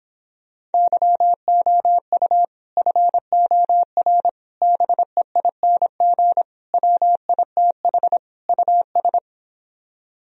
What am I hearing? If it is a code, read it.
Morse code, heard as a keyed tone, about 29 wpm, 700 Hz. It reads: YOU FOR BEING WITH US